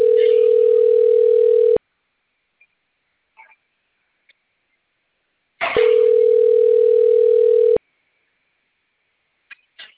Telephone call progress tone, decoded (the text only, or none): ringback tone